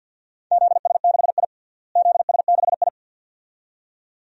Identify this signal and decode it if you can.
Morse code, heard as a keyed tone, about 52 wpm, 700 Hz. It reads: ZS6I ZS6I
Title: DXCC-100 Call Signs - 4 Characters - Repeated 2x - 50wpm